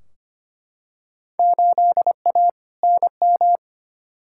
Morse code, heard as a keyed tone, about 25 wpm, 700 Hz. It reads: 8A NM
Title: Field Day Exchanges Encore - 25wpm